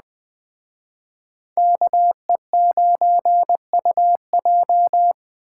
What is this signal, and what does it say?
Morse code, heard as a keyed tone, about 20 wpm, 700 Hz. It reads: KE9UJ
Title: Speed Racing - USA 2x2 Call Signs 17wpm